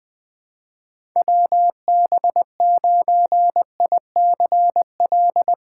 Morse code, heard as a keyed tone, about 20 wpm, 700 Hz. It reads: WB9ICL
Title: USA Call Signs - 6 Characters - 3x Character Spacing - 20wpm